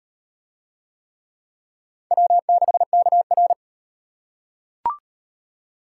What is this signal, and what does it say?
Morse code, heard as a keyed tone, about 36 wpm, 700 Hz. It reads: W6KR